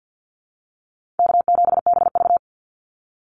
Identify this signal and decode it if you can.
Morse code, heard as a keyed tone, about 50 wpm, 700 Hz. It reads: K7BV
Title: Call Signs Repeated 2x - 50wpm